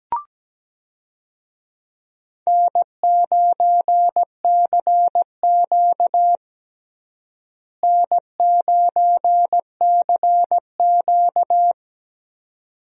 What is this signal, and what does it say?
Morse code, heard as a keyed tone, about 17 wpm, 700 Hz. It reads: N9CQ N9CQ
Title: USA Call Signs - 4 Characters - Repeated 2x - 17wpm